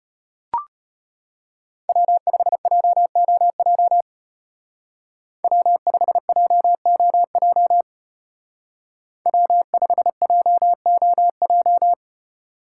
Morse code, heard as a keyed tone, about 31 wpm, 700 Hz. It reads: W5JOJ W5JOJ W5JOJ